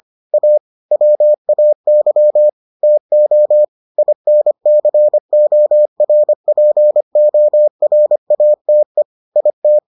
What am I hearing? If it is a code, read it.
Morse code, heard as a keyed tone, about 25 wpm, 600 Hz. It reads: A WAY TO INCORPORATE IT